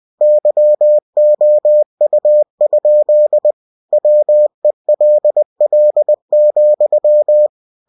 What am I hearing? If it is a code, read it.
Morse code, heard as a keyed tone, about 20 wpm, 600 Hz. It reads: YOU? WELL,
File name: cw-books / winnie-the-pooh / winnie-20wpm-page-0073